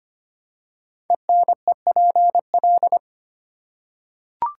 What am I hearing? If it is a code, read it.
Morse code, heard as a keyed tone, about 25 wpm, 700 Hz. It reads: ENEPL
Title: Code Groups - 5 Letters - 25wpm